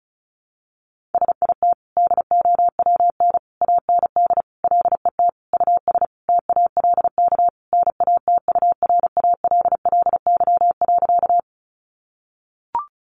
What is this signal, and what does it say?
Morse code, heard as a keyed tone, about 34 wpm, 700 Hz. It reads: SIT DOWN AND LET US TALK NATURALLY.
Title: Sentences from Top 1300 Words 35wpm